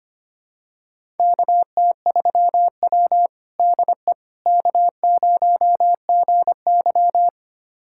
Morse code, heard as a keyed tone, about 25 wpm, 700 Hz. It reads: KT3W DE K0GY
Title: USA Callsign DE Callsign - 4 Characters - 25wpm